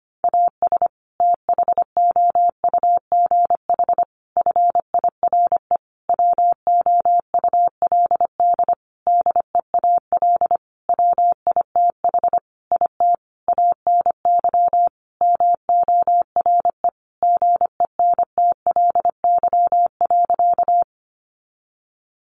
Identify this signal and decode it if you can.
Morse code, heard as a keyed tone, about 25 wpm, 700 Hz. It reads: AS THOUGH FIRE WOULD DEAL WITH IT ANY MORE GENTLY.
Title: Sentences from Top 1500 Words 25wpm